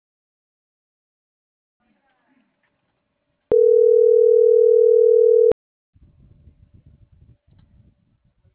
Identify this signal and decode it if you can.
Ringback tone (North America), 440 + 480 Hz.